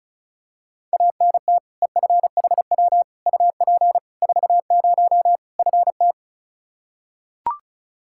Morse code, heard as a keyed tone, about 36 wpm, 700 Hz. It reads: ANT EFHW UP 40 FT
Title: Practice QSO - Exchange 2 - Info - 35wpm